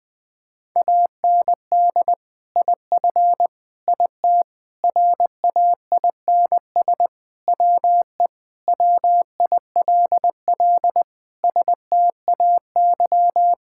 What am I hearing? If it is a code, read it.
Morse code, heard as a keyed tone, about 20 wpm, 700 Hz. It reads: AND IF IT RAINS WE WILL STAY
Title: Sentences with 2-Word Phrases - 20wpm